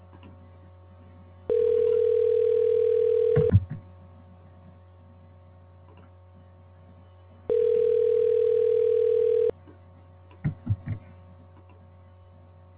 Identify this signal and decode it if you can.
Ringback tone (North America), 440 + 480 Hz.